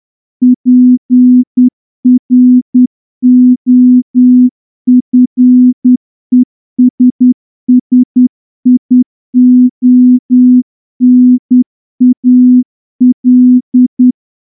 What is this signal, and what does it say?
Morse code, heard as a keyed tone, about 10 wpm, 250 Hz. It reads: PROFESSIONAL
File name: morse-training-10WPM